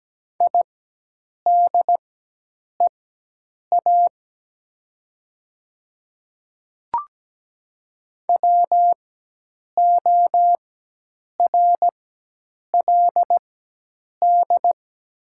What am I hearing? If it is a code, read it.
Morse code, heard as a keyed tone, about 17 wpm, spaced as for 8 wpm, 700 Hz. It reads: IDEA WORLD
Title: 4x Character Spacing - Top 500 Words - 17wpm